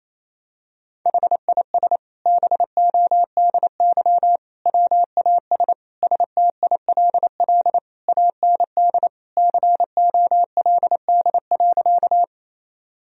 Morse code, heard as a keyed tone, about 28 wpm, 700 Hz. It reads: HIS BODY WAS STILL AND COLD.
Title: Sentences from Top 1000 Words - Review Part 2 - 28wpm